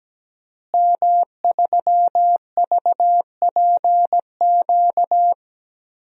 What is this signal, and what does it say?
Morse code, heard as a keyed tone, about 17 wpm, 700 Hz. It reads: M3VPQ
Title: DXCC-100 Call Signs - 5 Characters - Repeated 3x - 17wpm